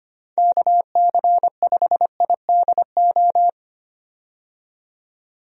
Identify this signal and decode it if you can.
Morse code, heard as a keyed tone, about 25 wpm, 700 Hz. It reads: KC5IDO
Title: USA Call Signs - 6 Characters - 25wpm